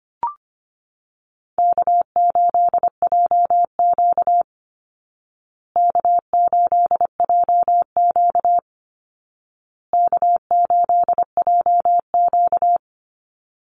Morse code, heard as a keyed tone, about 25 wpm, 700 Hz. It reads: K8JQ K8JQ K8JQ
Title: Easy Call Signs Repeated 3x - 25wpm